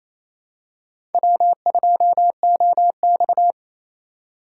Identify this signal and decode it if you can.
Morse code, heard as a keyed tone, about 28 wpm, 700 Hz. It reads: W2OX
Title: USA Call Signs - 3x Character Spacing - 28wpm